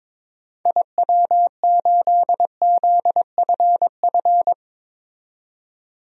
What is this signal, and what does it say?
Morse code, heard as a keyed tone, about 22 wpm, 700 Hz. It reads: IW8ZFF
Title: DXCC-100 Call Signs - 6 Characters - 22wpm